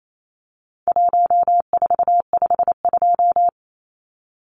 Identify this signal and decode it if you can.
Morse code, heard as a keyed tone, about 28 wpm, 700 Hz. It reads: 1452
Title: Numbers - Contest Style - 4-Digit 28wpm